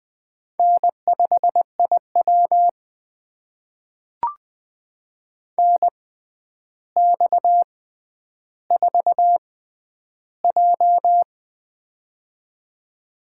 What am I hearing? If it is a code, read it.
Morse code, heard as a keyed tone, about 20 wpm, spaced as for 10 wpm, 700 Hz. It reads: N5IW NX4J
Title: USA Call Signs - 4 Characters - 6x Character Spacing - 20wpm